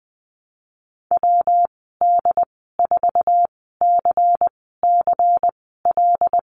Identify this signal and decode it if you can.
Morse code, heard as a keyed tone, about 20 wpm, spaced as for 14 wpm, 700 Hz. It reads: WD4CCL
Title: 140 Lesson 23g - Letter f - Call Signs 20wpm